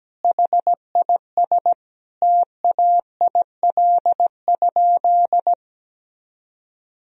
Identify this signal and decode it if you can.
Morse code, heard as a keyed tone, about 17 wpm, 700 Hz. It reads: HIS TAIL?
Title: Sentences from Top 1000 Words - Review Part 3 - 17wpm